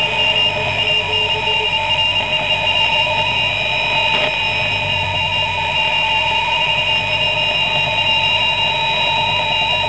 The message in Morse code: none